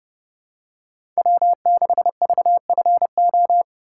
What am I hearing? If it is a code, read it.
Morse code, heard as a keyed tone, about 30 wpm, 700 Hz. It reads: W6VFO
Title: USA Call Signs - 5 Characters - 2x Character Spacing - 30wpm